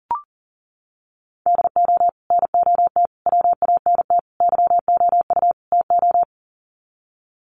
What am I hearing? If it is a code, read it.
Morse code, heard as a keyed tone, about 40 wpm, 700 Hz. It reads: DO NOT WANT YOU TO
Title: Sets of 5 Words 40wpm